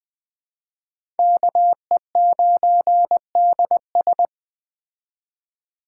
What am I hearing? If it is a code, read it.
Morse code, heard as a keyed tone, about 20 wpm, 700 Hz. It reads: KE9DS
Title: Speed Racing - USA Call Signs 17wpm